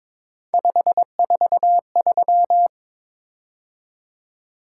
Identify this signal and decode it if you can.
Morse code, heard as a keyed tone, about 22 wpm, 700 Hz. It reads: 543
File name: practice / 22wpm / Three Letter-Number Combinations - 22wpm